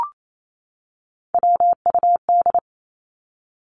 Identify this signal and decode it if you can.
Morse code, heard as a keyed tone, about 28 wpm, 700 Hz. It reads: WUD
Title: Abbreviations 28wpm